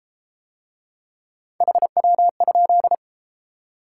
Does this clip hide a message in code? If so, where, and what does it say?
Yes: Morse code from 1.6 to 3.0 s, heard as a keyed tone, about 31 wpm, 700 Hz. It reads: HW?